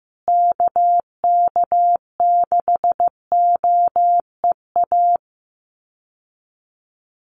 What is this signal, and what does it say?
Morse code, heard as a keyed tone, about 15 wpm, 700 Hz. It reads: KK6OEA
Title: USA Call Signs Repeated 2x 15wpm